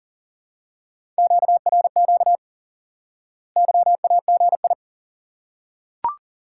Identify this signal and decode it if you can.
Morse code, heard as a keyed tone, about 40 wpm, 700 Hz. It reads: QRQ YAGI
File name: IQE Course 3 - Lesson 89 - YAGI - Two Elements - 40wpm